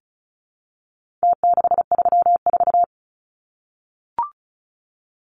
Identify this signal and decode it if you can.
Morse code, heard as a keyed tone, about 34 wpm, 700 Hz. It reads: T634
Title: Numbers - Contest Style - 4-Digit 35wpm